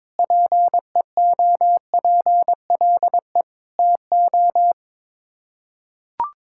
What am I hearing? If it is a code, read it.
Morse code, heard as a keyed tone, about 22 wpm, 700 Hz. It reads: PEOPLE TO
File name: practice / 22wpm / Sets of 7 Words - 22wpm